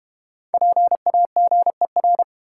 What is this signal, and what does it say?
Morse code, heard as a keyed tone, about 31 wpm, 700 Hz. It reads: PAGER
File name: practice / 25wpm / Difficult Words - Speed-Racing with Hints - 25wpm